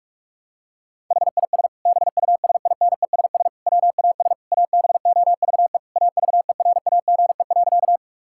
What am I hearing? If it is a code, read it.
Morse code, heard as a keyed tone, about 41 wpm, 700 Hz. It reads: HIS BUSINESS WAS ABOVE AVERAGE.
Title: Sentences from Top 1000 Words - Review Part 6 - 45wpm